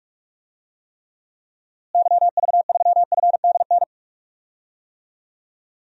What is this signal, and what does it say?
Morse code, heard as a keyed tone, about 43 wpm, 700 Hz. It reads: YV3FDN
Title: DXCC-100 Call Signs - 6 Characters - Repeated 3x - 45wpm